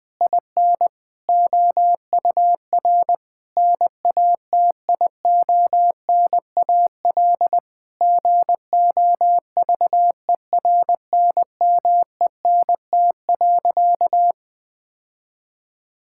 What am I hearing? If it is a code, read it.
Morse code, heard as a keyed tone, about 21 wpm, 700 Hz. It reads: IN OUR NATIONAL GOVERNMENT.
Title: Sentences from Top 1000 Words - Review Part 6 - 20wpm